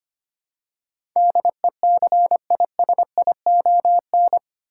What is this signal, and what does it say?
Morse code, heard as a keyed tone, about 25 wpm, 690 Hz. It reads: DECISION
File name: 8 Letter Words - First 3 Letters Given - 25wpm